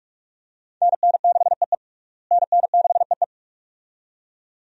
Morse code, heard as a keyed tone, about 47 wpm, 700 Hz. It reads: NN6EE NN6EE
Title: USA Call Signs - 5 Characters - Repeated 2x - 45wpm